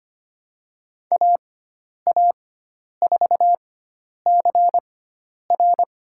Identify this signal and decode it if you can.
Morse code, heard as a keyed tone, about 25 wpm, spaced as for 10 wpm, 700 Hz. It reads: AA4CR
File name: Lesson 16g - Letter c - Call Signs 25wpm